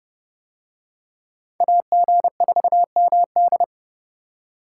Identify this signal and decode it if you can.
Morse code, heard as a keyed tone, about 29 wpm, 700 Hz. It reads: AG4MD